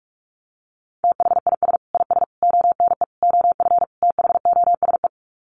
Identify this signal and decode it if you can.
Morse code, heard as a keyed tone, about 44 wpm, 700 Hz. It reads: THIS IS ONE OF THOSE
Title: Sets of 5 Words Encore2 45wpm